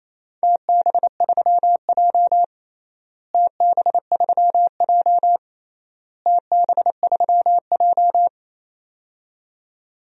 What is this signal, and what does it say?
Morse code, heard as a keyed tone, about 28 wpm, 700 Hz. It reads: TB3J TB3J TB3J
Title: DXCC-100 Call Signs - 4 Characters - Repeated 3x - 28wpm